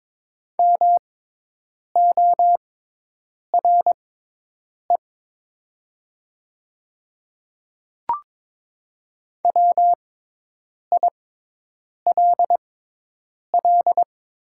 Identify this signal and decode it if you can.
Morse code, heard as a keyed tone, about 21 wpm, spaced as for 8 wpm, 700 Hz. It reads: MORE WILL
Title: 6x Character Spacing - Top 100 Words - 22wpm